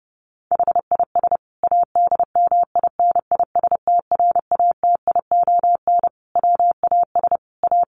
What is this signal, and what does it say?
Morse code, heard as a keyed tone, about 30 wpm, 700 Hz. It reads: HIS ADMINISTRATION WAS A
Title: Sentences from Top 1300 Words 30wpm